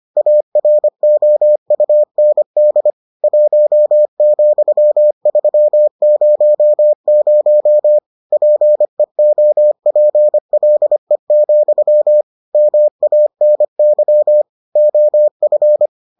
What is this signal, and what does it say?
Morse code, heard as a keyed tone, about 25 wpm, 600 Hz. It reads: AROUND 1,300 PEOPLE, MANY OF